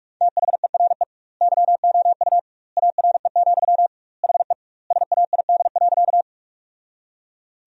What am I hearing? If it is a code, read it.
Morse code, heard as a keyed tone, about 46 wpm, 700 Hz. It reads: THERE YOU ARE, HE SAID.
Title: 2x Word-Spacing - Sentences from Top 100 Words - 45wpm